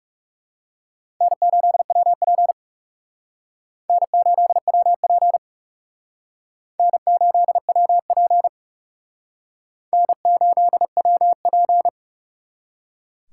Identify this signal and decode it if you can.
Morse code, heard as a keyed tone, about 37 wpm, 700 Hz. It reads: N8WP N8WP N8WP N8WP
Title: Speed Racing - USA 1x2 Call Signs 30wpm